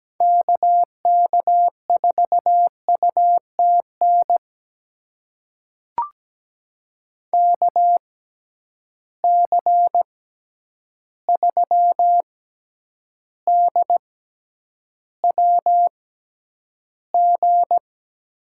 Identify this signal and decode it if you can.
Morse code, heard as a keyed tone, about 17 wpm, spaced as for 9 wpm, 700 Hz. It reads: KK4UTN KC3DWG
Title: USA Call Signs - 6 Characters - 6x Character Spacing - 17wpm